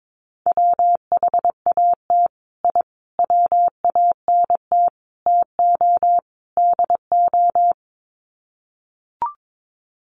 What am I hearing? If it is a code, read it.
Morse code, heard as a keyed tone, about 22 wpm, 700 Hz. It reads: WHAT I WANT TO DO
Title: Sets of 5 Words Encore 22wpm